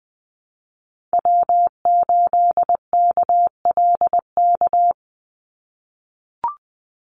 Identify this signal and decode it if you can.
Morse code, heard as a keyed tone, about 20 wpm, 700 Hz. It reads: W8KLK
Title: USA Call Signs 20wpm